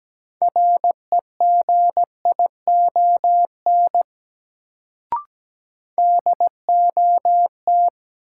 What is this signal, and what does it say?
Morse code, heard as a keyed tone, about 17 wpm, 700 Hz. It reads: REGION DOT
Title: Sentences from Top 1500 Words - New Word Drill Alt - 17wpm